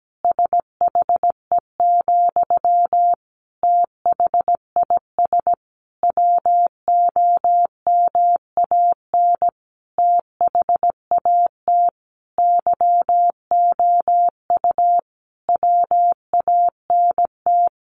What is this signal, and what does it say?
Morse code, heard as a keyed tone, about 17 wpm, 700 Hz. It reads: SHE, THIS WOMAN THAT YOU WANT
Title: Sentences from Top 1100 Words 17wpm